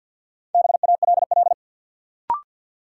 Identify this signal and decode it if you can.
Morse code, heard as a keyed tone, about 52 wpm, 700 Hz. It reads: BALL
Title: Sentences from Top 900 Words - New Word Drill - 50wpm